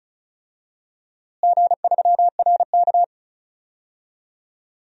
Morse code, heard as a keyed tone, about 34 wpm, 700 Hz. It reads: G3RK
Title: DXCC-100 Call Signs - 4 Characters - Repeated 2x - 35wpm